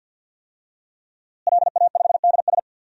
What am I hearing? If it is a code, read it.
Morse code, heard as a keyed tone, about 52 wpm, 700 Hz. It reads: LA5DS
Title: DXCC-100 Call Signs - 5 Characters - Repeated 3x - 50wpm